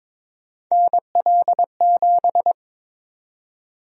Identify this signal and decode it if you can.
Morse code, heard as a keyed tone, about 22 wpm, 700 Hz. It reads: NL7
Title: USA Call Sign Prefixes - 22wpm